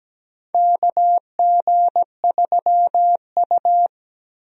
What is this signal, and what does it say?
Morse code, heard as a keyed tone, about 17 wpm, 700 Hz. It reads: KG3U
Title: USA Call Signs - 4 Characters - Repeated 2x - 17wpm